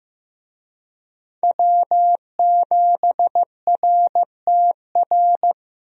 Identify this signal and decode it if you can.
Morse code, heard as a keyed tone, about 16 wpm, 700 Hz. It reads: W7RTR
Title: USA Call Signs - 5 Characters - Repeated 3x - 15wpm